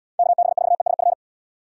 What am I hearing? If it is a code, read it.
Morse code, heard as a keyed tone, about 69 wpm, 700 Hz. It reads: KB4ESF